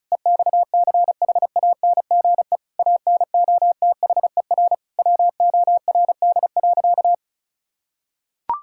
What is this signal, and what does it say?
Morse code, heard as a keyed tone, about 36 wpm, 700 Hz. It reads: EXCHANGE ANOTHER WORD.
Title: Sentences from Top 1000 Words - Review Part 6 - 35wpm